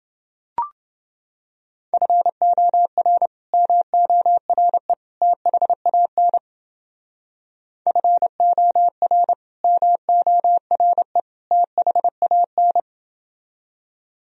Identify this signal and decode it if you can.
Morse code, heard as a keyed tone, about 30 wpm, 700 Hz. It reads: FOR MORE THAN FOR MORE THAN